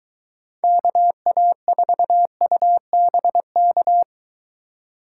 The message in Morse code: KA4UBK

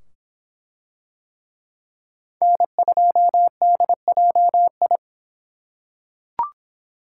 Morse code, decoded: N2DJI

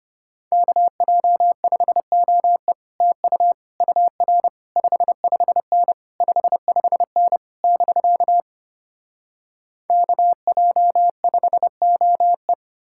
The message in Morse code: KJ5OE TU UR 55N 55N <BK> KJ5OE

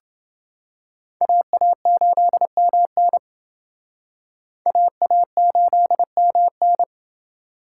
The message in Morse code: AA8MN AA8MN

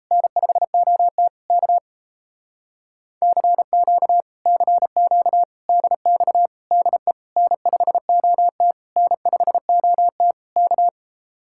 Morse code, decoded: N5OT K CQ CQ DX DE N5OT N5OT K

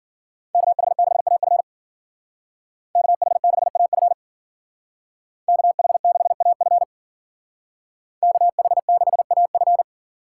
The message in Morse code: KH6AF KH6AF KH6AF KH6AF

